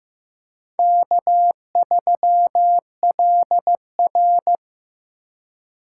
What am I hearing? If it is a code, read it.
Morse code, heard as a keyed tone, about 15 wpm, 700 Hz. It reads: K3LR
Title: K1USN SST QSO Practice - 15wpm